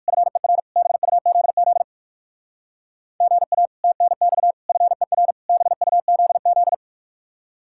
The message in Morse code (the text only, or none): FER BUZZ GA TNX FER BUZZ